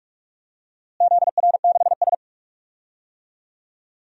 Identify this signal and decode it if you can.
Morse code, heard as a keyed tone, about 42 wpm, 700 Hz. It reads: ZR6S
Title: DXCC-100 Call Signs - 4 Characters - Repeated 2x - 45wpm